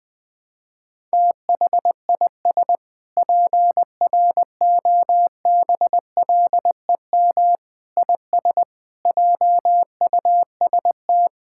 Morse code, heard as a keyed tone, about 20 wpm, 700 Hz. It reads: THIS PROBLEM IS JUST
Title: Sentences with 7-Word Phrases - 20wpm